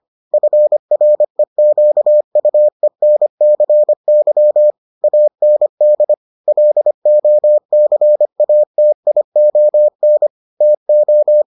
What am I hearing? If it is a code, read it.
Morse code, heard as a keyed tone, about 25 wpm, 600 Hz. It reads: FREQUENCY AND LOCATION TO